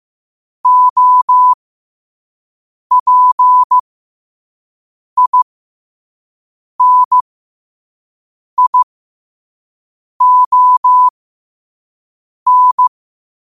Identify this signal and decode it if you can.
Morse code, heard as a keyed tone, about 15 wpm, spaced as for 5 wpm, 1000 Hz. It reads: OPINION
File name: qotdfiles / 5wpm